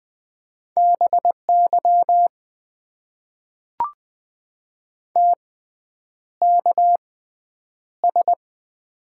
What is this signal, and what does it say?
Morse code, heard as a keyed tone, about 20 wpm, spaced as for 9 wpm, 700 Hz. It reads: BY TKS
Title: Top 90 QSO Elements - 6x Character Spacing - 20wpm